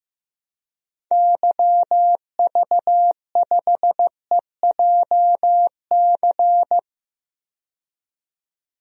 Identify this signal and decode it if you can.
Morse code, heard as a keyed tone, about 15 wpm, 700 Hz. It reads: YV5EJC
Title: DXCC-100 Call Signs - 6 Characters - 15wpm